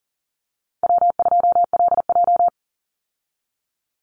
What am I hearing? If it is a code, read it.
Morse code, heard as a keyed tone, about 40 wpm, 700 Hz. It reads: W2LJ
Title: Easy Call Signs Repeated 3x - 40wpm